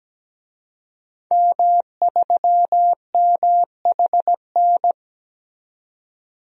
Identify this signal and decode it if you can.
Morse code, heard as a keyed tone, about 17 wpm, 700 Hz. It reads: M3MHN
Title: DXCC-100 Call Signs - 5 Characters - 17wpm